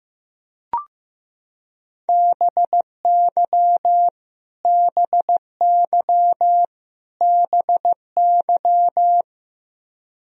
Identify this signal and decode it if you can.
Morse code, heard as a keyed tone, about 15 wpm, 700 Hz. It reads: BY BY BY
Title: Top 100 Words Repeated 15wpm